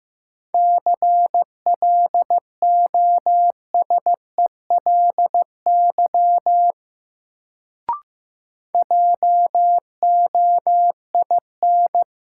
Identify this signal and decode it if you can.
Morse code, heard as a keyed tone, about 15 wpm, 700 Hz. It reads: CLOSELY JOIN
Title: Sentences from Top 1400 Words - New Word Drill Alt - 15wpm